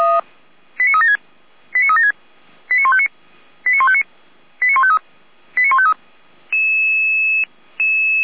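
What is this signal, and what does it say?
Five-tone selective call (ZVEI), heard as quick runs of five tones, tones of about 70 ms. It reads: 89376 89376 89159 89159 89153 89153
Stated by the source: ZVEI five-tone selective call 89160 received on SDR, from Feuerwehr Orlamünde audio archive.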